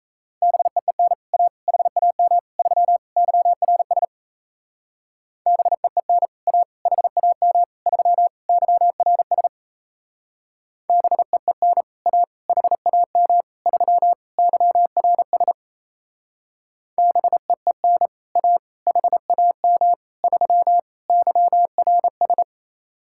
Morse code, heard as a keyed tone, about 33 wpm, 700 Hz. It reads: BEEN A HAM 3 YRS BEEN A HAM 3 YRS BEEN A HAM 3 YRS BEEN A HAM 3 YRS
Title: Speed-Racing Common QSO Phrases Encore - 28wpm